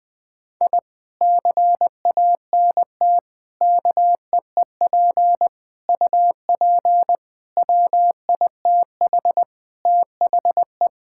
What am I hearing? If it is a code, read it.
Morse code, heard as a keyed tone, about 20 wpm, 700 Hz. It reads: I CANT KEEP UP WITH THE
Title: Sentences with 2-Word Phrases - Spoken Beginning - 20wpm